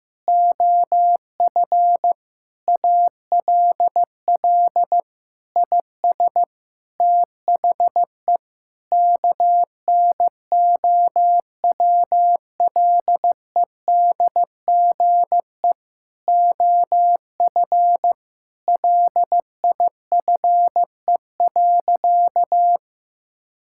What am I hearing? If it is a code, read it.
Morse code, heard as a keyed tone, about 16 wpm, 700 Hz. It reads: OF ALL IS THE KNOWLEDGE OF LIFE.
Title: Sentences from Top 1000 Words - Review Part 6 - 15wpm